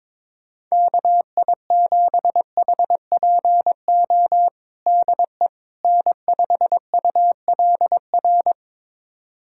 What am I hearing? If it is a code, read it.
Morse code, heard as a keyed tone, about 22 wpm, 700 Hz. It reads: KI7HPO DE N5ULR
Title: USA Callsign DE Callsign 22wpm